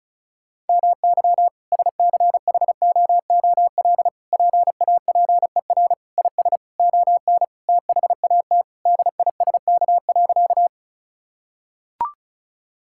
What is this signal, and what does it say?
Morse code, heard as a keyed tone, about 36 wpm, 700 Hz. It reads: MY SCHOOL PAPER IS ON THAT DISK.
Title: Sentences from Top 1000 Words - Review Part 6 - 35wpm